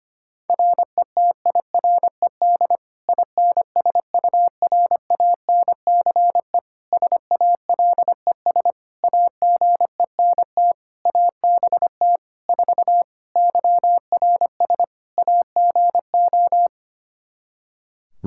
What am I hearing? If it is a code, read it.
Morse code, heard as a keyed tone, about 25 wpm, 700 Hz. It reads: RETIRED INSURANCE SALES AGENT ABT 4 YRS AGO
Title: K1USN SST QSO Practice 25wpm